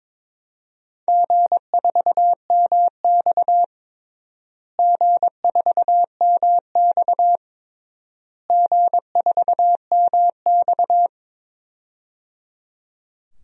Morse code, heard as a keyed tone, about 22 wpm, 700 Hz. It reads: G4MX G4MX G4MX
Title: DXCC-100 Call Signs - 4 Characters - Repeated 3x - 22wpm